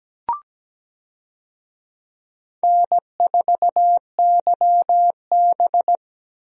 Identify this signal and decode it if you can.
Morse code, heard as a keyed tone, about 17 wpm, 700 Hz. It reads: N4YB
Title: USA Call Signs - 4 Characters - Repeated 3x - 17wpm